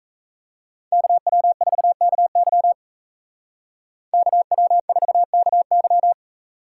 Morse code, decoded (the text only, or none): KW4KY KW4KY